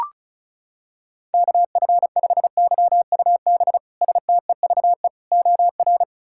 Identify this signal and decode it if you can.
Morse code, heard as a keyed tone, about 36 wpm, 700 Hz. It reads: KF5YUB STEVE OR
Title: K1USN SST QSO Practice - 35wpm